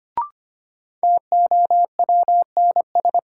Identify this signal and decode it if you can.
Morse code, heard as a keyed tone, about 25 wpm, 700 Hz. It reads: TOWNS
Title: Sentences from Top 1000 Words - Review Part 6 - New Word Drill - 25wpm